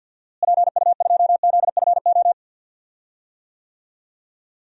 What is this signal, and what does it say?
Morse code, heard as a keyed tone, about 47 wpm, 700 Hz. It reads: PU2ZFO
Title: DXCC-100 Call Signs - 6 Characters - Repeated 2x - 50wpm